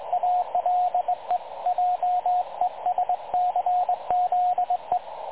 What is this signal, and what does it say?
Morse code, heard as a keyed tone, about 21 wpm, 720 Hz, with radio noise. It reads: ALE JESCZE
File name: dlazaawansowanych / nagraniazpasma / websdr recording 3565.0kHz